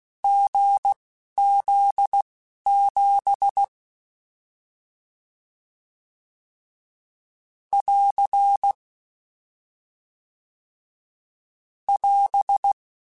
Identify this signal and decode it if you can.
Morse code, heard as a keyed tone, about 16 wpm, spaced as for 11 wpm, 800 Hz. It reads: GZ7 <AR> <AS>
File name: Difficult Combos